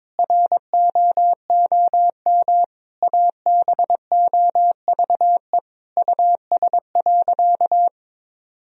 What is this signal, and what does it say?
Morse code, heard as a keyed tone, about 22 wpm, 700 Hz. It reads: ROOM ABOVE US.
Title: Sentences from Top 1000 Words - Review Part 3 - 22wpm